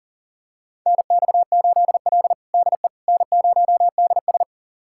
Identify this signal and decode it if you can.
Morse code, heard as a keyed tone, about 37 wpm, 700 Hz. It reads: NX8L DE N0DS
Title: USA Callsign DE Callsign - 4 Characters - 40wpm